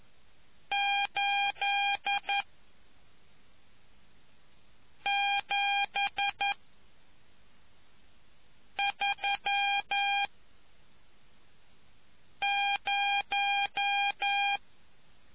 Morse code, heard as a keyed tone, about 11 wpm, 800 Hz. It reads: 8 7 3 0